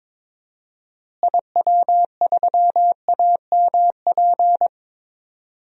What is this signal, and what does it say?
Morse code, heard as a keyed tone, about 22 wpm, 700 Hz. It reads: IW3AMP
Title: DXCC-100 Call Signs - 6 Characters - Repeated 3x - 22wpm